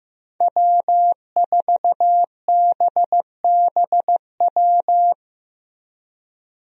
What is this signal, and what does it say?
Morse code, heard as a keyed tone, about 15 wpm, 700 Hz. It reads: W4BBW
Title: USA Call Signs - 5 Characters - Repeated 3x - 15wpm